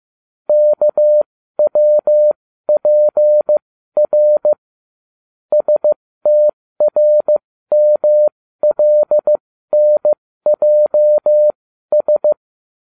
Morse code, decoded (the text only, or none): KWPR STRMLNJS